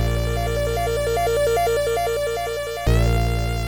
There is no code.